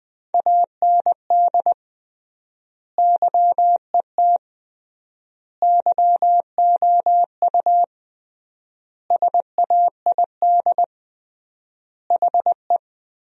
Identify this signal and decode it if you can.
Morse code, heard as a keyed tone, about 20 wpm, 700 Hz. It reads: AND YET YOU SAID HE
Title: 3x Word-Spacing - Sentences from Top 500 Words - 20wpm